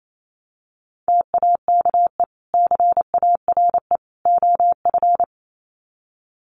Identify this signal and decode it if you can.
Morse code, heard as a keyed tone, about 28 wpm, 700 Hz. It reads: TAKE CARE OF
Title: Sets of 3 Words 28wpm